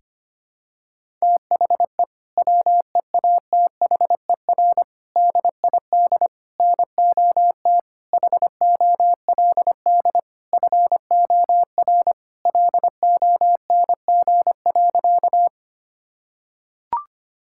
Morse code, heard as a keyed tone, about 25 wpm, 700 Hz. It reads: THE WEATHER DID NOT HOLD FOR LONG.
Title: Sentences from Top 1000 Words - Review Part 5 - 25wpm